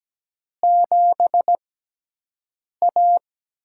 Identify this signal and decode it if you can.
Morse code, heard as a keyed tone, about 17 wpm, spaced as for 6 wpm, 700 Hz. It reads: 7A
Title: USA Call Signs - 4 Characters - 6x Character Spacing - 17wpm